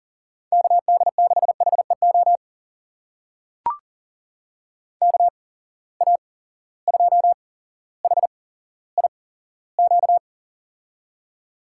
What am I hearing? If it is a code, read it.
Morse code, heard as a keyed tone, about 43 wpm, spaced as for 17 wpm, 700 Hz. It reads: KD6HEO KA2HIQ
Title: USA Call Signs - 6 Characters - 8x Character Spacing - 40wpm